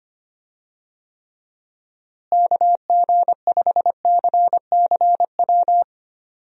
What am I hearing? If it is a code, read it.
Morse code, heard as a keyed tone, about 25 wpm, 700 Hz. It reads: KG5CCW
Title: USA Call Signs - 6 Characters - Repeated 3x - 25wpm